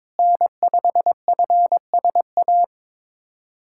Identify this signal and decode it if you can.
Morse code, heard as a keyed tone, about 22 wpm, 700 Hz. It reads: N5FSA
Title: USA Callsign DE Callsign - 5 Characters - 22wpm